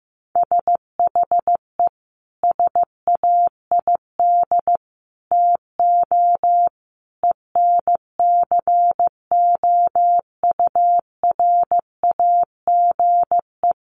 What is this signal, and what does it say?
Morse code, heard as a keyed tone, about 15 wpm, 700 Hz. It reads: SHE SAID TO ENCOURAGE
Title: Sentences from Top 1400 Words 15wpm